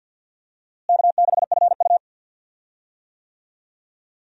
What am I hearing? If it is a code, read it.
Morse code, heard as a keyed tone, about 52 wpm, 700 Hz. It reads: K6FU
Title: USA Call Signs - 4 Characters - Repeated 3x - 50wpm